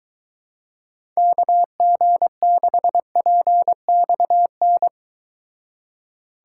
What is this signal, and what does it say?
Morse code, heard as a keyed tone, about 23 wpm, 700 Hz. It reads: KG6PXN